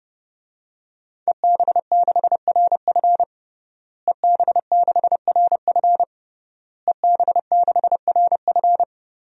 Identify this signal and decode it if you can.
Morse code, heard as a keyed tone, about 32 wpm, 700 Hz. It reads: EB6RF EB6RF EB6RF
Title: DXCC-100 Call Signs - 5 Characters - Repeated 3x - 30wpm